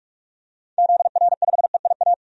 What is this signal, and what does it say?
Morse code, heard as a keyed tone, about 45 wpm, 700 Hz. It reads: ZR5EIA
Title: DXCC-100 Call Signs - 6 Characters - 45wpm